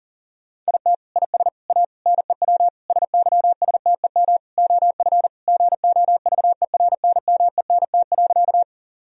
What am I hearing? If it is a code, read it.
Morse code, heard as a keyed tone, about 40 wpm, 700 Hz. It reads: IT IS A NEW SYSTEM OF GOVERNMENT.
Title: Sentences from Top 1000 Words - Review Part 3 - 40wpm